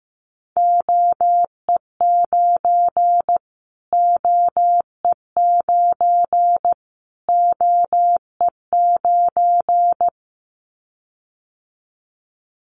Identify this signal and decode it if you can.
Morse code, heard as a keyed tone, about 15 wpm, 700 Hz. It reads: OE9 OE9 OE9
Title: Callsign Prefixes for Top 25 DXCC - Repeated 3x - 15wpm